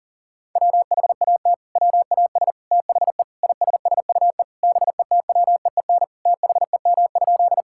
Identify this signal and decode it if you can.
Morse code, heard as a keyed tone, about 37 wpm, 700 Hz. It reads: WHAT WAS THE ISSUE BETWEEN THEM?
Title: Sentences from Top 1000 Words - Review Part 6 - 40wpm